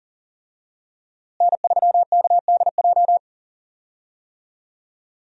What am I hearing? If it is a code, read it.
Morse code, heard as a keyed tone, about 40 wpm, 700 Hz. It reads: N3KDJ
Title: USA Call Signs - 5 Characters - Repeated 2x - 40wpm